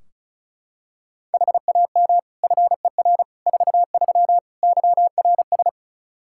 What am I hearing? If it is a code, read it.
Morse code, heard as a keyed tone, about 36 wpm, 700 Hz. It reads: HAM FER 43 YRS
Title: K1USN SST QSO Practice 35wpm